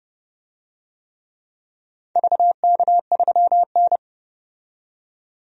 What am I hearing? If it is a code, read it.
Morse code, heard as a keyed tone, about 30 wpm, 700 Hz. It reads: VK3N
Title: DXCC-100 Call Signs - 4 Characters - 30wpm